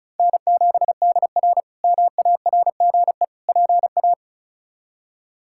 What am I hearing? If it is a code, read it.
Morse code, heard as a keyed tone, about 36 wpm, 700 Hz. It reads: N7DR MARGE PA
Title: K1USN SST QSO Practice - 35wpm